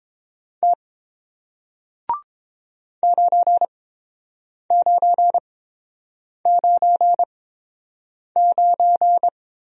Morse code, heard as a keyed tone, about 30 wpm, spaced as for 7 wpm, 700 Hz. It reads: T 9999